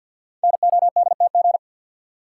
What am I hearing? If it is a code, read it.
Morse code, heard as a keyed tone, about 50 wpm, 700 Hz. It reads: NODTG
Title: Code Groups - 5 Letters - 50wpm